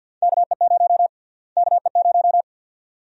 Morse code, heard as a keyed tone, about 52 wpm, 700 Hz. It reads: KE0 KE0
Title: USA Call Sign Prefixes - Repeated 2x - 50wpm